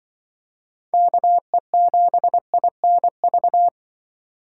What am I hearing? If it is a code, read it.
Morse code, heard as a keyed tone, about 24 wpm, 700 Hz. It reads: KE7INV
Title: Speed Racing - USA 2x3 Call Signs 20wpm